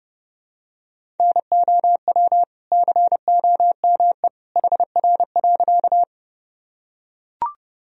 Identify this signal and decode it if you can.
Morse code, heard as a keyed tone, about 32 wpm, 700 Hz. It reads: NOW COME HR.
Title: Sentences from Top 200 Words and Abbreviations 30wpm